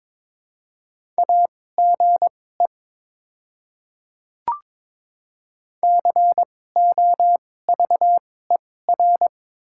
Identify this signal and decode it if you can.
Morse code, heard as a keyed tone, about 22 wpm, spaced as for 16 wpm, 700 Hz. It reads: AGE COVER
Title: 2x Character Spacing - Top 500 Words - 22wpm